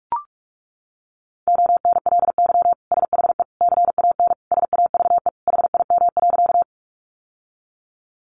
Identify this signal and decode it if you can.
Morse code, heard as a keyed tone, about 44 wpm, 700 Hz. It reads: ONLY SHE CAN SAVE HIM.
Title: Sentences from Top 1100 Words 45wpm